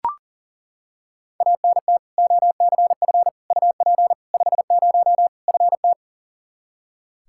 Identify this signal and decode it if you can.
Morse code, heard as a keyed tone, about 40 wpm, 700 Hz. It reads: ANT OCF UP 50 FT
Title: Practice QSO - Exchange 2 - Info - 40wpm